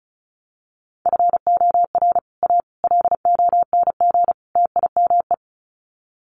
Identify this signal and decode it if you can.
Morse code, heard as a keyed tone, about 34 wpm, 700 Hz. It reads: FOR A LONG TIME
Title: Sets of 4 Words 35wpm